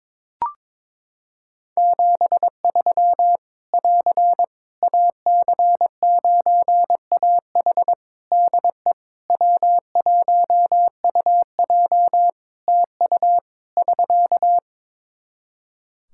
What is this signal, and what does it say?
Morse code, heard as a keyed tone, about 21 wpm, 700 Hz. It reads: 73 <AR> AC9AH DE W1UJ TU <SK>
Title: K1USN SST QSO Practice 22wpm